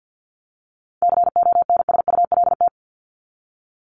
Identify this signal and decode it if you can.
Morse code, heard as a keyed tone, about 48 wpm, 700 Hz. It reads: CONSULT
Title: Top 1751-2000 Words 50wpm